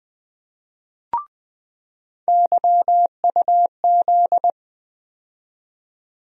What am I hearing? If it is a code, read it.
Morse code, heard as a keyed tone, about 21 wpm, 700 Hz. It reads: YUZ
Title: USA Call Sign Postfixes - 3 Letters 20wpm